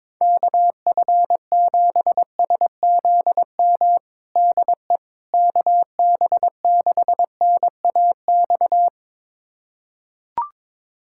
Morse code, heard as a keyed tone, about 22 wpm, 700 Hz. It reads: KF7SZM DE KB6NAX
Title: USA Callsign DE Callsign - 6 Characters - 22wpm